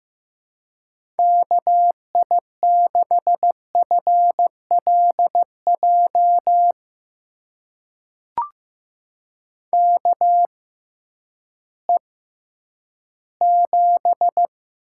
Morse code, heard as a keyed tone, about 15 wpm, spaced as for 10 wpm, 700 Hz. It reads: KI6FLJ KE7
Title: USA Call Signs - 6x Character Spacing - 15wpm